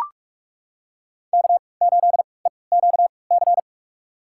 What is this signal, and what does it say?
Morse code, heard as a keyed tone, about 42 wpm, spaced as for 25 wpm, 700 Hz. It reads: K8EQC
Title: USA Call Signs - 3x Character Spacing - 45wpm